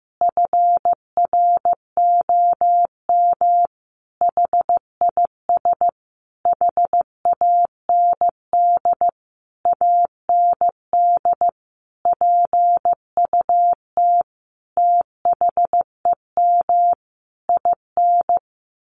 Morse code, heard as a keyed tone, about 15 wpm, 700 Hz. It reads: FROM HIS HAND AND PUT THEM IN
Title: Sentences from Top 1300 Words 15wpm